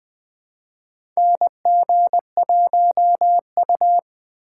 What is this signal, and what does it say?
Morse code, heard as a keyed tone, about 20 wpm, 700 Hz. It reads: NG1U
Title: USA Call Signs - 4 Characters - 4x Character Spacing - 20wpm